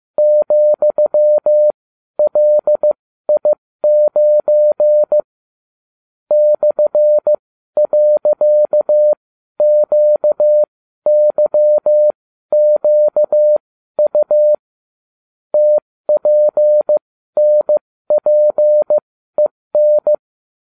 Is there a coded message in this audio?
Yes: Morse code from 0.2 to 20.1 s, heard as a keyed tone, about 15 wpm, spaced as for 12 wpm, 600 Hz. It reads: ,LI9 /.QYQU TPNPEN